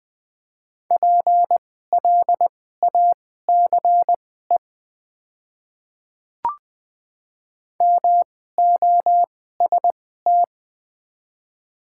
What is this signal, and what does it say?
Morse code, heard as a keyed tone, about 20 wpm, spaced as for 14 wpm, 700 Hz. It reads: PLACE MOST
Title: 2x Character Spacing - Top 500 Words - 20wpm